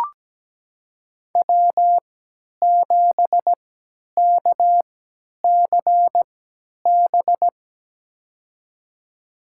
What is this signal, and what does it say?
Morse code, heard as a keyed tone, about 17 wpm, spaced as for 10 wpm, 700 Hz. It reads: W7KCB
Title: USA Call Signs - 3x Character Spacing - 17wpm